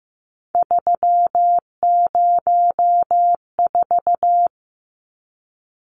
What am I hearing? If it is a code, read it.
Morse code, heard as a keyed tone, about 15 wpm, 700 Hz. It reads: 304
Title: Numbers - 3-Digit 15wpm